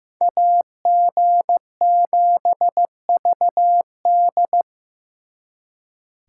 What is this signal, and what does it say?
Morse code, heard as a keyed tone, about 18 wpm, 700 Hz. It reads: AG7VD
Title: Speed Racing - USA Call Signs 15wpm